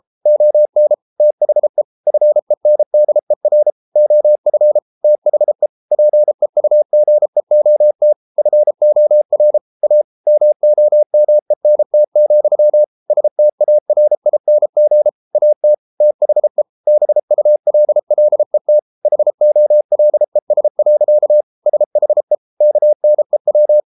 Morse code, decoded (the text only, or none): ON THE FENDER OF THE PEUGEOT FOR A MOMENT, STARING AT THE BULLET HOLES. SHE KNEW